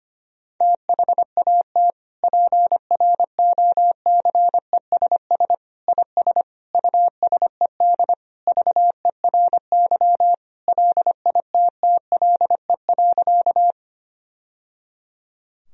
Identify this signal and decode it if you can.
Morse code, heard as a keyed tone, about 25 wpm, 700 Hz. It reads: THAT PROCESS IS USED VERY LITTLE.
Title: Sentences from Top 1000 Words - Review Part 3 - 25wpm